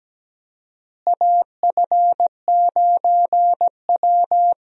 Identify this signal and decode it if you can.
Morse code, heard as a keyed tone, about 17 wpm, 700 Hz. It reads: AF9W
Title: USA Call Signs - 4 Characters - Repeated 2x - 17wpm